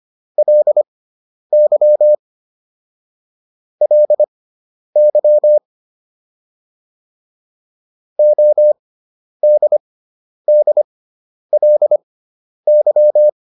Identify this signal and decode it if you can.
Morse code, heard as a keyed tone, about 25 wpm, spaced as for 10 wpm, 600 Hz. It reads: LY LY ODDLY